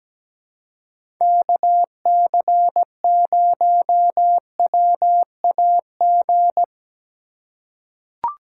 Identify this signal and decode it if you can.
Morse code, heard as a keyed tone, about 17 wpm, 700 Hz. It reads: KC0WAG
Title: USA Call Signs - 3x Character Spacing - 17wpm